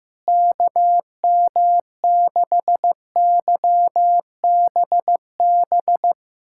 Morse code, heard as a keyed tone, about 15 wpm, 700 Hz. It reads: KM6YBB
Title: USA Callsign DE Callsign - 6 Characters - 15wpm